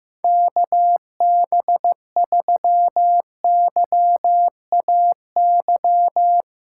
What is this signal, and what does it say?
Morse code, heard as a keyed tone, about 15 wpm, 700 Hz. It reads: KB3YAY
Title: USA Callsign DE Callsign - 6 Characters - 15wpm